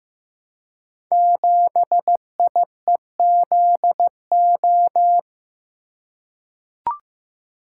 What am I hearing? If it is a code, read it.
Morse code, heard as a keyed tone, about 15 wpm, 700 Hz. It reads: 7IEZO
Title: Code Groups - 5 Letter:Numbers - 15wpm